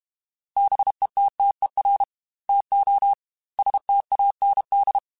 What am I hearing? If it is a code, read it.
Morse code, heard as a keyed tone, about 31 wpm, 800 Hz. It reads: BETTER TO STAND
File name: Quotes / Quotes-mp3s / Temper @30